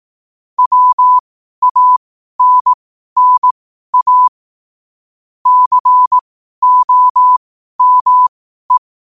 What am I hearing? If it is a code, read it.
Morse code, heard as a keyed tone, about 18 wpm, spaced as for 12 wpm, 1000 Hz. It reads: WANNA COME
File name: qotdfiles / 12wpm